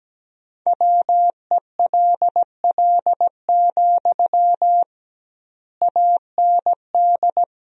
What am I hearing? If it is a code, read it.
Morse code, heard as a keyed tone, about 17 wpm, 700 Hz. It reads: WELL, AND
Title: 2x Word-Spacing - Sentences from Top 200 Words - 17wpm